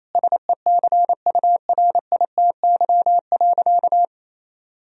Morse code, heard as a keyed tone, about 28 wpm, 700 Hz. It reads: SECURITY.
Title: Sentences from Top 1000 Words - Review Part 4 - 28wpm